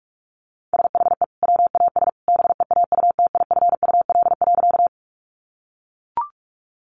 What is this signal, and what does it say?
Morse code, heard as a keyed tone, about 44 wpm, 700 Hz. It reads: SHE WAS BEAUTIFUL.
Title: Sentences from Top 900 Words 45wpm